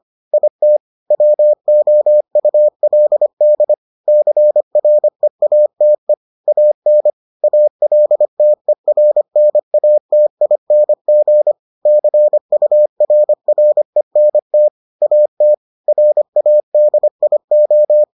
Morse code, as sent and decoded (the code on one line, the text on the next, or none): .. - / .-- --- ..- .-.. -.. / -.-. .-. .- - . / .- -. / .- .-.. - . .-. -. .- - .. -. --. / -.-. ..- .-. .-. -. - / .- - / .-. .- -.. .. ---
IT WOULD CREATE AN ALTERNATING CURRENT AT RADIO